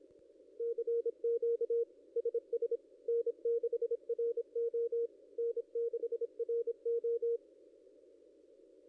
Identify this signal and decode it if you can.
Morse code, heard as a keyed tone, about 26 wpm, 470 Hz. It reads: CQ SS N6RO N6RO
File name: contests / ss-cw-k5tr-op-n5rz -0900